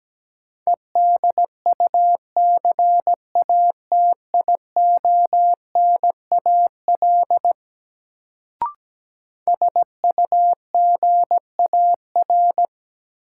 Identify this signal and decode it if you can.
Morse code, heard as a keyed tone, about 17 wpm, 700 Hz. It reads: EDUCATIONAL SUGAR
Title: Sentences from Top 1200 Words - New Word Drill - 17wpm